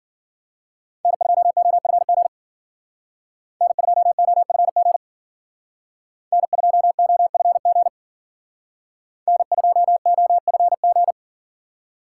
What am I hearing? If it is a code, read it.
Morse code, heard as a keyed tone, about 52 wpm, 700 Hz. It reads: N2OFG N2OFG N2OFG N2OFG